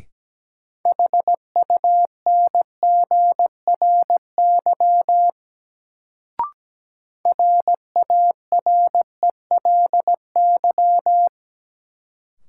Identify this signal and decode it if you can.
Morse code, heard as a keyed tone, about 17 wpm, 700 Hz. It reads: HUNGRY RARELY
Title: Sentences from Top 1400 Words - New Word Drill - 17wpm